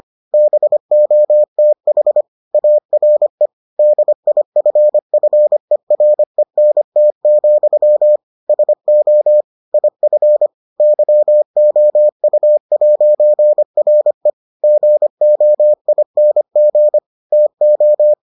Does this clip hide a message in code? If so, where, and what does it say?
Yes: Morse code from 0.3 to 18.1 s, heard as a keyed tone, about 25 wpm, 600 Hz. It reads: BOTH ARE DIFFERENT, SO IF YOU'RE GOING TO